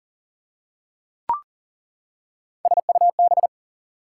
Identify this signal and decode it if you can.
Morse code, heard as a keyed tone, about 37 wpm, 700 Hz. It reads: SUB
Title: Prefixes and Suffixes 40wpm